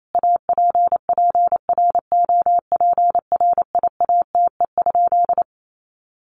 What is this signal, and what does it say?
Morse code, heard as a keyed tone, about 28 wpm, 700 Hz. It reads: APPROPRIATE?
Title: Sentences from Top 900 Words 28wpm